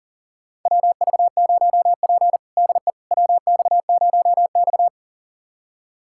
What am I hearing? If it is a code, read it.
Morse code, heard as a keyed tone, about 40 wpm, 700 Hz. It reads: WV0P DE WX0X
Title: USA Callsign DE Callsign - 4 Characters - 40wpm